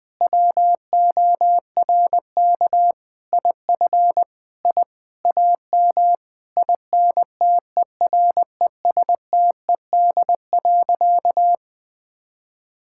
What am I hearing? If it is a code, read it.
Morse code, heard as a keyed tone, about 20 wpm, 700 Hz. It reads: WORK IF I AM INTERESTED.
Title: Sentences from Top 1000 Words - Review Part 6 - 20wpm